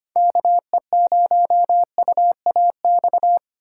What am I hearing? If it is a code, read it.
Morse code, heard as a keyed tone, about 33 wpm, 700 Hz. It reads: KE0UAX